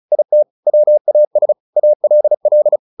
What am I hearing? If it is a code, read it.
Morse code, heard as a keyed tone, about 34 wpm, 600 Hz. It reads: IT WAS ALL
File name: the old man and the sea / the old man and the sea 1 35wpm